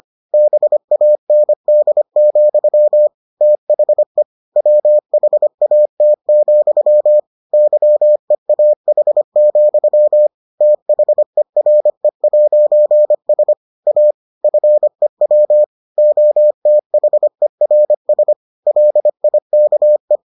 Morse code, heard as a keyed tone, about 25 wpm, 600 Hz. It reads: BAND, THE WHAT, YEAH, THERE'S A FEW OTHERS LIKE